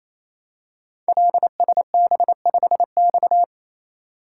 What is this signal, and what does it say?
Morse code, heard as a keyed tone, about 28 wpm, 700 Hz. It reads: LSB5X
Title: Code Groups - 5 Letter:Numbers - 28wpm